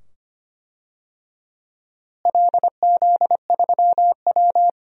Morse code, heard as a keyed tone, about 25 wpm, 700 Hz. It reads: LZ3W
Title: DXCC-100 Call Signs - 4 Characters - Repeated 2x - 25wpm